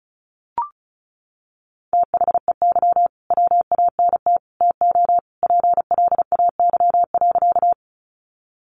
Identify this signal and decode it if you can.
Morse code, heard as a keyed tone, about 34 wpm, 700 Hz. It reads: THEY WANT TO PLAY.
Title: Easy Phrases 35wpm